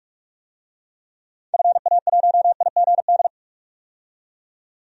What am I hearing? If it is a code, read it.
Morse code, heard as a keyed tone, about 46 wpm, 700 Hz. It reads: FA1IGD
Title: DXCC-100 Call Signs - 6 Characters - 45wpm